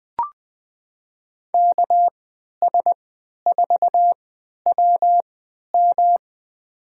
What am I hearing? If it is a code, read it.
Morse code, heard as a keyed tone, about 20 wpm, spaced as for 11 wpm, 700 Hz. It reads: KS4WM
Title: USA Call Signs - 5 Characters - 3x Character Spacing - 20wpm